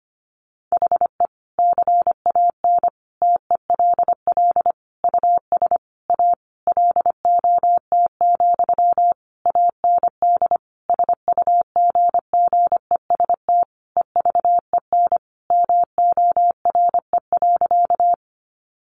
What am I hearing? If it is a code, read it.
Morse code, heard as a keyed tone, about 25 wpm, 700 Hz. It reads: HE CAN TELL US A LOT, AND SUGGEST EVEN MORE.
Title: Sentences from Top 1300 Words 25wpm